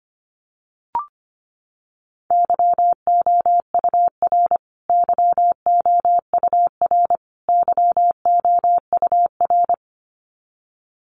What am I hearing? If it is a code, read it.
Morse code, heard as a keyed tone, about 25 wpm, 700 Hz. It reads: YOUR YOUR YOUR
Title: Top 500 Words Repeated 25wpm